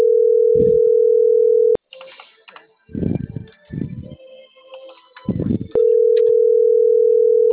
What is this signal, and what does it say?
Ringback tone (North America), 440 + 480 Hz.